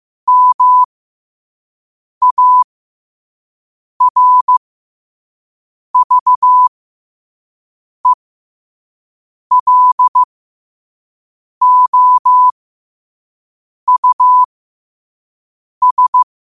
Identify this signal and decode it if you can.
Morse code, heard as a keyed tone, about 15 wpm, spaced as for 5 wpm, 1000 Hz. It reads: MARVELOUS
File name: qotdfiles / 5wpm